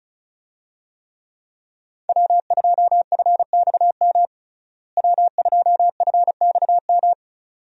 Morse code, heard as a keyed tone, about 36 wpm, 700 Hz. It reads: W2FXM W2FXM
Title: USA Call Signs - 5 Characters - Repeated 2x - 35wpm